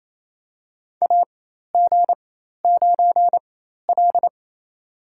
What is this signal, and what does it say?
Morse code, heard as a keyed tone, about 28 wpm, spaced as for 13 wpm, 700 Hz. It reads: AG9L
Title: USA Call Signs - 4 Characters - 4x Character Spacing - 28wpm